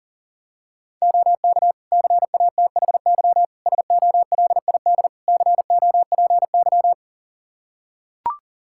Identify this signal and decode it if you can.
Morse code, heard as a keyed tone, about 37 wpm, 700 Hz. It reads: OK CATHY SOLID COPY
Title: K1USN SST QSO Practice 40wpm